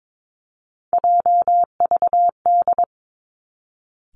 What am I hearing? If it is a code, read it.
Morse code, heard as a keyed tone, about 22 wpm, 700 Hz. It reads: JVD
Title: Three Challenging Letter-Number Combinations 22wpm